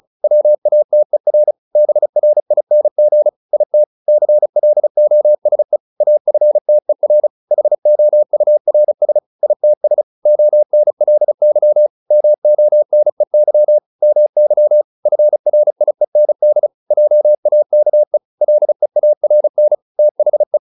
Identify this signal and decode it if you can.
Morse code, heard as a keyed tone, about 34 wpm, 600 Hz. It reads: WATER BRING IT CLOSE AFTER HOURS ITS ONLY MONEY MY FRIEND JAKE LEARN THE
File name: PR401 35